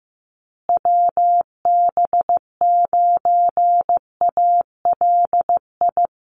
Morse code, heard as a keyed tone, about 15 wpm, 700 Hz. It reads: WB9ALI
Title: USA Call Signs Repeated 3x 15wpm